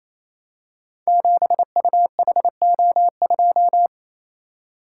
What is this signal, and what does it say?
Morse code, heard as a keyed tone, about 27 wpm, 700 Hz. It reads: 7UHO2
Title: Code Groups - 5 Letter:Numbers - 28wpm